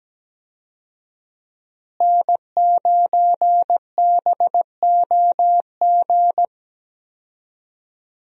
Morse code, heard as a keyed tone, about 17 wpm, 700 Hz. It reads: N9BOG
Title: USA Call Signs - 5 Characters - Repeated 3x - 17wpm